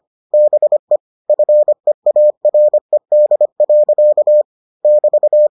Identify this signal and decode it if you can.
Morse code, heard as a keyed tone, about 25 wpm, 600 Hz. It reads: BE FEARED. <BT>